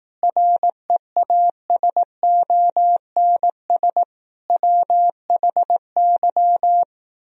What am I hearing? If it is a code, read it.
Morse code, heard as a keyed tone, about 18 wpm, 700 Hz. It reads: REASONS WHY
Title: Speed-Racing - Sets of 5 Words 15wpm